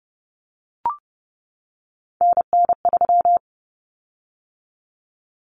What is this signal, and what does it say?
Morse code, heard as a keyed tone, about 30 wpm, 700 Hz. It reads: NN3
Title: Common Cut Numbers - 3-Digit 30wpm